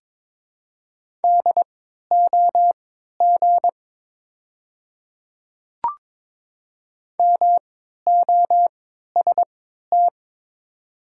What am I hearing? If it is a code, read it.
Morse code, heard as a keyed tone, about 22 wpm, spaced as for 12 wpm, 700 Hz. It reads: DOG MOST
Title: 3x Character Spacing - Top 500 Words - 22wpm